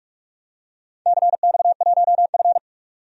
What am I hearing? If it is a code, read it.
Morse code, heard as a keyed tone, about 41 wpm, 700 Hz. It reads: CX1F
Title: DXCC-100 Call Signs - 4 Characters - 45wpm